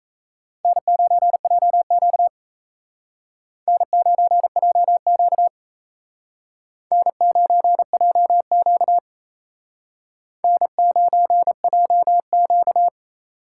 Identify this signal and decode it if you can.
Morse code, heard as a keyed tone, about 35 wpm, 700 Hz. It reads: N9JQ N9JQ N9JQ N9JQ
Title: Speed Racing - USA 1x2 Call Signs 28wpm